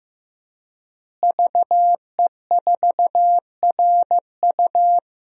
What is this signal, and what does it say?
Morse code, heard as a keyed tone, about 15 wpm, 700 Hz. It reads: VE4RU
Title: DXCC-100 Call Signs - 5 Characters - Repeated 3x - 15wpm